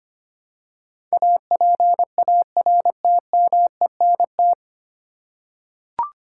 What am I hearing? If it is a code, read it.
Morse code, heard as a keyed tone, about 25 wpm, 700 Hz. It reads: APARTMENT
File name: practice / 25wpm / 9 Letter Words - First 3 Letters Given - 25wpm